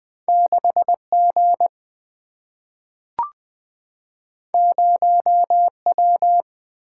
Morse code, heard as a keyed tone, about 20 wpm, 700 Hz. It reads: 6 G 0 W
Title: Single-Letter Single-Number 20wpm